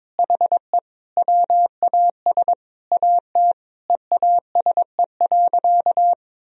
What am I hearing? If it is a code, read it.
Morse code, heard as a keyed tone, about 22 wpm, 700 Hz. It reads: HE WAS AT EASE.
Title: Sentences from Top 1000 Words - Review Part 1 - 22wpm